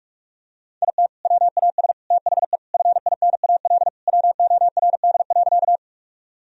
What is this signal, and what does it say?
Morse code, heard as a keyed tone, about 41 wpm, 700 Hz. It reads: IT WAS THE FINAL WORD.
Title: Sentences from Top 1000 Words - Review Part 2 - 45wpm